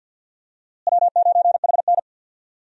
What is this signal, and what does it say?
Morse code, heard as a keyed tone, about 50 wpm, 700 Hz. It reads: W9HN
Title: USA Call Signs - 4 Characters - Repeated 3x - 50wpm